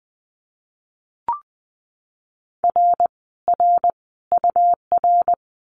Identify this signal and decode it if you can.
Morse code, heard as a keyed tone, about 20 wpm, 700 Hz. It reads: R R UR
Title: Common QSO Phrases Encore 20wpm